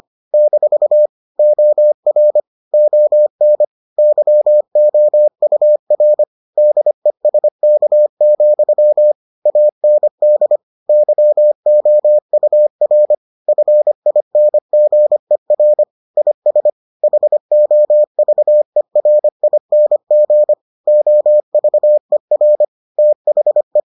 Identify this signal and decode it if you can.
Morse code, heard as a keyed tone, about 25 wpm, 600 Hz. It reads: - OR ON YOUR DESK, AND YOUR FINGER IS HOVERING OVER THE